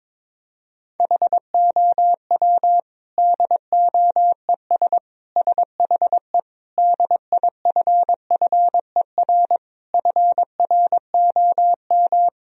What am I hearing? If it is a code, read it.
Morse code, heard as a keyed tone, about 22 wpm, 700 Hz. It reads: HOW DOES SHE DIFFER FROM
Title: Sentences from Top 1000 Words - Review Part 1 - 22wpm